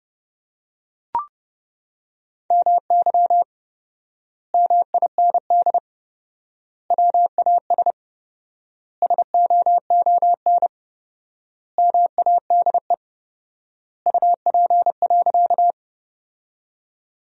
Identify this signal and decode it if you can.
Morse code, heard as a keyed tone, about 30 wpm, 700 Hz. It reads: MY MIND WAS SOON MADE UP.
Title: 4x Word-Spacing - Sentences from Top 500 Words - 30wpm